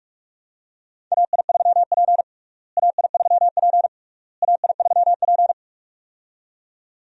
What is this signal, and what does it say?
Morse code, heard as a keyed tone, about 48 wpm, 700 Hz. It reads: AI3P AI3P AI3P
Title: USA Call Signs - 4 Characters - Repeated 3x - 45wpm